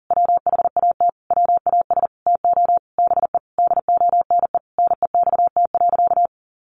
Morse code, heard as a keyed tone, about 40 wpm, 700 Hz. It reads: WHAT WAS TO BE DONE NEXT.
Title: Sentences from Top 1200 Words 40wpm